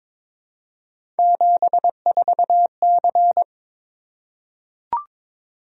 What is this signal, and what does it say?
Morse code, heard as a keyed tone, about 22 wpm, 700 Hz. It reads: 74C
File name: practice / 22wpm / Three Letter-Number Combinations - 22wpm